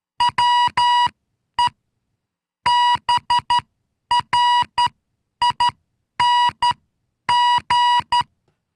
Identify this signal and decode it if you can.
Morse code, heard as a keyed tone, about 12 wpm, spaced as for 9 wpm, 970 Hz. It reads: WE BRING